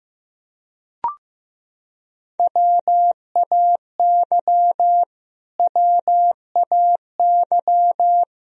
Morse code, heard as a keyed tone, about 15 wpm, 700 Hz. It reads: WAY WAY
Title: Top 100 Words Repeated 15wpm